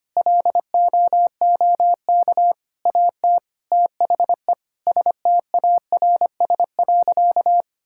Morse code, heard as a keyed tone, about 25 wpm, 700 Hz. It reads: LOOK AT THE STARS.
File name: practice / 25wpm / Sentences with 4-Word Phrases - 25wpm